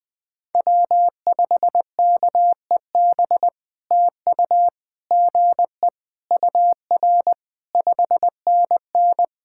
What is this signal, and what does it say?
Morse code, heard as a keyed tone, about 19 wpm, 700 Hz. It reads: W5KEB TU GE UR 5NN
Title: POTA QSO Practice - 20wpm